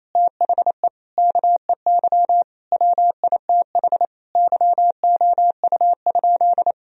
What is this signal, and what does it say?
Morse code, heard as a keyed tone, about 28 wpm, 700 Hz. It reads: THE KEY WITH YOU?
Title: Sentences from Top 1000 Words - Review Part 4 - 28wpm